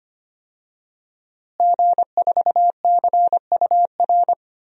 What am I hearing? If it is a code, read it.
Morse code, heard as a keyed tone, about 25 wpm, 700 Hz. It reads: G4CUR
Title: DXCC-100 Call Signs - 5 Characters - Repeated 2x - 25wpm